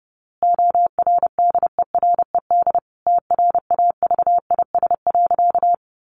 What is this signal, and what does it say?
Morse code, heard as a keyed tone, about 28 wpm, 700 Hz. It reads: ORDERED TRAVIS.
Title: A Sound Of Thunder 30wpm